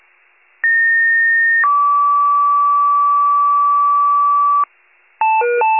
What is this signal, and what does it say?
Two-tone page, heard as a one-second tone, then a three-second tone, about 1850 then 1150 Hz. It is heard from 0.6 to 4.6 s.